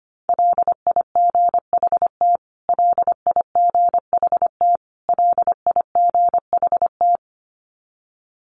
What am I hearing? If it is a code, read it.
Morse code, heard as a keyed tone, about 25 wpm, 700 Hz. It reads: LIGHT LIGHT LIGHT
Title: Top 500 Words Repeated 25wpm